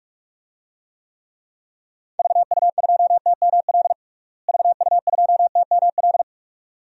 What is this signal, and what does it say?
Morse code, heard as a keyed tone, about 41 wpm, 700 Hz. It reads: VU2TML VU2TML
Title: DXCC-100 Call Signs - 6 Characters - Repeated 2x - 45wpm